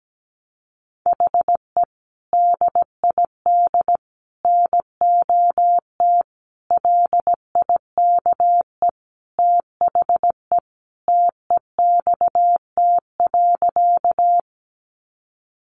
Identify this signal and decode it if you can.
Morse code, heard as a keyed tone, about 17 wpm, 700 Hz. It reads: HE DID NOT LIKE THE TEXT.
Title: Sentences from Top 1200 Words 17wpm